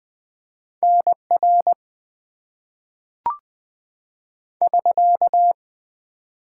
Morse code, heard as a keyed tone, about 20 wpm, 700 Hz. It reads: NR <SK>
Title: Top 90 QSO Elements - 4x Character Spacing - 20wpm